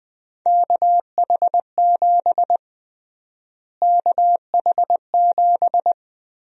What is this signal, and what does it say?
Morse code, heard as a keyed tone, about 20 wpm, 700 Hz. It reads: KH7 KH7
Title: USA Call Sign Prefixes - Repeated 2x - 20wpm